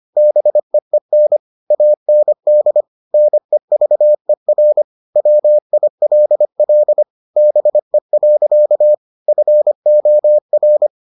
Morse code, heard as a keyed tone, about 25 wpm, 600 Hz. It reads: BEEN AND NEVER WILL BE. FOR